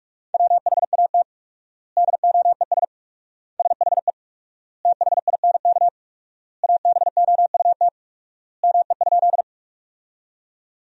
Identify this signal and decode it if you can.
Morse code, heard as a keyed tone, about 46 wpm, 700 Hz. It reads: WHAT DOES SHE THINK ABOUT ME?
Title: 4x Word-Spacing - Sentences from Top 200 Words - 45wpm